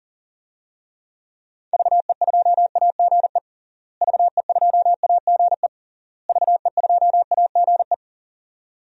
Morse code, heard as a keyed tone, about 37 wpm, 700 Hz. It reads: VE2AGE VE2AGE VE2AGE
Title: DXCC-100 Call Signs - 6 Characters - Repeated 3x - 40wpm